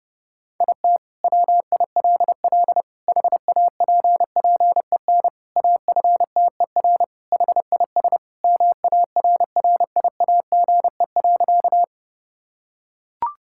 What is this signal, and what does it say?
Morse code, heard as a keyed tone, about 30 wpm, 700 Hz. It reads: IT WILL HAPPEN AFTER HIS MARRIAGE.
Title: Sentences from Top 1000 Words - Review Part 5 - 30wpm